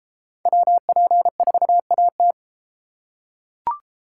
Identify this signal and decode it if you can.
Morse code, heard as a keyed tone, about 32 wpm, 700 Hz. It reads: WP4AT